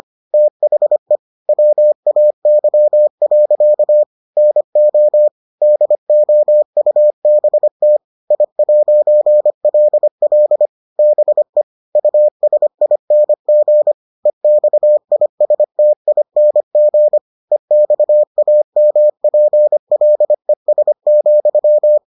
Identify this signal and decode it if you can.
Morse code, heard as a keyed tone, about 25 wpm, 600 Hz. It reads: THE WAY. NO DOUBT I'LL BE USING EXISTING EXAMPLES,